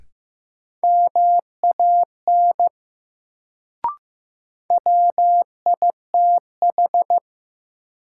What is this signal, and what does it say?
Morse code, heard as a keyed tone, about 15 wpm, 700 Hz. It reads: MAN WITH
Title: Sets of 2 Words - New Word Drill - 15wpm